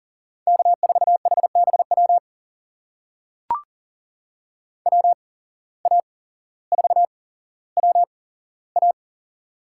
Morse code, heard as a keyed tone, about 40 wpm, spaced as for 17 wpm, 700 Hz. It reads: K4HBW WA4WA
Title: USA Call Signs - 5 Characters - 8x Character Spacing - 40wpm